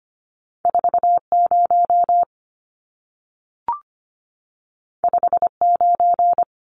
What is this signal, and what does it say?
Morse code, heard as a keyed tone, about 25 wpm, 700 Hz. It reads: 40 59
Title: Numbers - 2-Digit 25wpm